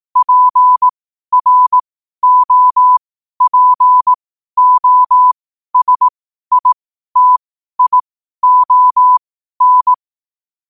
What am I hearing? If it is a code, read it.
Morse code, heard as a keyed tone, about 18 wpm, spaced as for 12 wpm, 1000 Hz. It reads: PROPOSITION
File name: qotdfiles / 12wpm